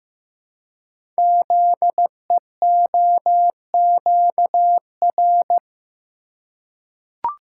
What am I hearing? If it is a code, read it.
Morse code, heard as a keyed tone, about 15 wpm, 700 Hz. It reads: ZEOQR
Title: Code Groups - 5 Letters - 15wpm